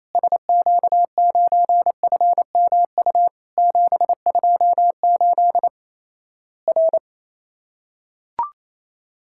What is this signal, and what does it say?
Morse code, heard as a keyed tone, about 28 wpm, 700 Hz. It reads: SQ9FMU 728 R
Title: Worked All Europe DX Contest - QTC Exchange - 28wpm